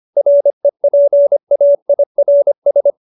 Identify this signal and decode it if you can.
Morse code, heard as a keyed tone, about 25 wpm, 570 Hz. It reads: REPAIRS